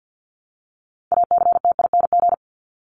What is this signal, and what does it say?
Morse code, heard as a keyed tone, about 50 wpm, 700 Hz. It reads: ACTING
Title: ING Suffix - Top 100 Words with Base Words 50wpm